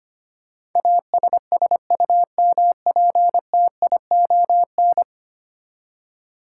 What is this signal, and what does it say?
Morse code, heard as a keyed tone, about 25 wpm, 700 Hz. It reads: ASSUMPTION
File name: Ten Letter Words - 25wpm